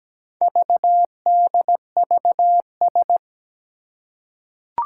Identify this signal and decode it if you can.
Morse code, heard as a keyed tone, about 17 wpm, 700 Hz. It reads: VDVS
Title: Four Challenging Letter-Number Combinations 17wpm